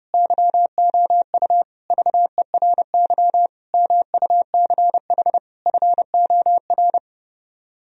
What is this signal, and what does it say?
Morse code, heard as a keyed tone, about 34 wpm, 700 Hz. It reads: YOU VERY MUCH FOR
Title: Speed-Racing - Sets of 5 Words 30wpm